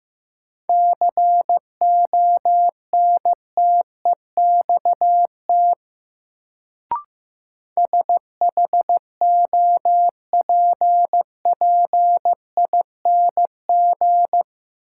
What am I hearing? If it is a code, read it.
Morse code, heard as a keyed tone, about 15 wpm, 700 Hz. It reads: CONTEXT SHOPPING
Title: Sentences from Top 1000 Words - Review Part 7 - New Word Drill - 15wpm